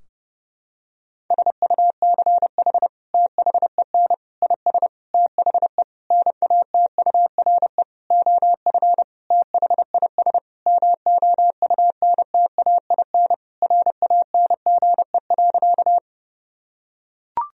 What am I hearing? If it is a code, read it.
Morse code, heard as a keyed tone, about 30 wpm, 700 Hz. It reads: SUCH THEN IS THE NATURE OF THIS MOUNTAIN RANGE.
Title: Sentences from Top 1000 Words - Review Part 4 - 30wpm